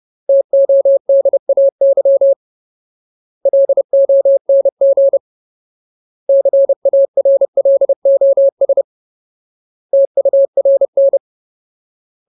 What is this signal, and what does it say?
Morse code, heard as a keyed tone, about 30 wpm, 550 Hz. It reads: TODAY LONG CARLOS TURN
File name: Ultimate Word List / 30WPM Ultimate